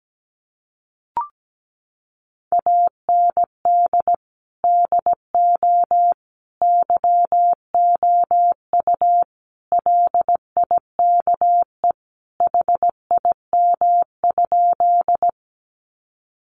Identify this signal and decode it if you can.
Morse code, heard as a keyed tone, about 17 wpm, 700 Hz. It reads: AND DO YOU LIKE HIM?
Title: Sentences from Top 100 Words 17wpm